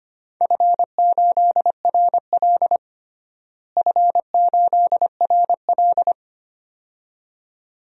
Morse code, heard as a keyed tone, about 25 wpm, 700 Hz. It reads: F8RL F8RL
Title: DXCC-100 Call Signs - 4 Characters - Repeated 2x - 25wpm